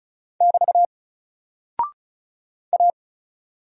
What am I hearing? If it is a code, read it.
Morse code, heard as a keyed tone, about 36 wpm, 700 Hz. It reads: <BT> A
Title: Single Character (Letter:Number:Punctuation:Prosign) 35wpm